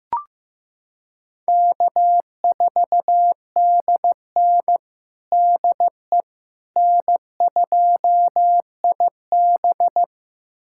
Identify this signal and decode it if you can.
Morse code, heard as a keyed tone, about 15 wpm, 700 Hz. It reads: K4DN DE N2IB
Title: USA Callsign DE Callsign - 4 Characters - 15wpm